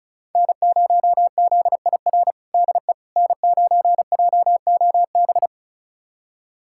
Morse code, heard as a keyed tone, about 36 wpm, 700 Hz. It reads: N0ZIR DE N9JOB
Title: USA Callsign DE Callsign - 5 Characters - 35wpm